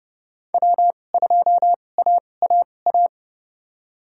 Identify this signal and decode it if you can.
Morse code, heard as a keyed tone, about 30 wpm, spaced as for 22 wpm, 700 Hz. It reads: W2AAA
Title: USA Call Signs - 2x Character Spacing - 30wpm